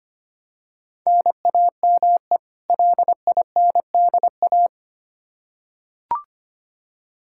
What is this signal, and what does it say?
Morse code, heard as a keyed tone, about 25 wpm, 700 Hz. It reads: NAME LINDA
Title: Practice QSO - Exchange 1 - Beginning - 25wpm